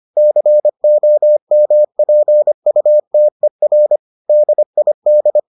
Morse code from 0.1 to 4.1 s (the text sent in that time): COMPUTER